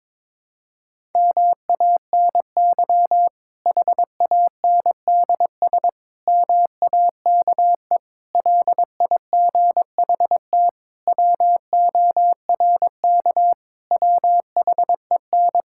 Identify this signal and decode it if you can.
Morse code, heard as a keyed tone, about 22 wpm, 700 Hz. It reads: MANY HANDS MAKE LIGHT WORK WHEN
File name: Difficult Sentences with Hints - 22wpm